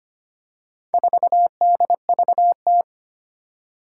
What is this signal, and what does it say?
Morse code, heard as a keyed tone, about 25 wpm, 700 Hz. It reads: 4DVT
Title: Four Challenging Letter-Number Combinations 25wpm